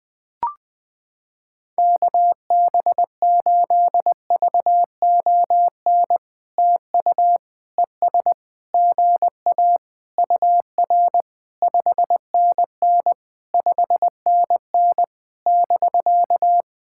Morse code, heard as a keyed tone, about 21 wpm, 700 Hz. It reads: KB8VON TU ES GA UR 5NN 5NN <BK>
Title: POTA QSO Practice - 20wpm